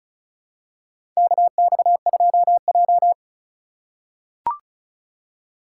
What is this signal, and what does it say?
Morse code, heard as a keyed tone, about 36 wpm, 700 Hz. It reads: KX2J
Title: USA Call Signs - 4 Characters - 2x Character Spacing - 35wpm